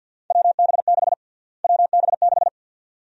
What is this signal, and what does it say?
Morse code, heard as a keyed tone, about 49 wpm, 700 Hz. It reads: WB6 WB6
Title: USA Call Sign Prefixes - Repeated 2x - 50wpm